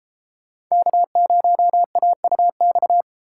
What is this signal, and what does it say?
Morse code, heard as a keyed tone, about 32 wpm, 700 Hz. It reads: K0AUX